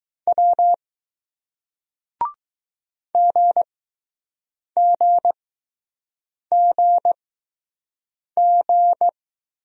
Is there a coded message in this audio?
Yes: Morse code from 0.3 to 9.1 s, heard as a keyed tone, about 20 wpm, spaced as for 6 wpm, 700 Hz. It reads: W GGGG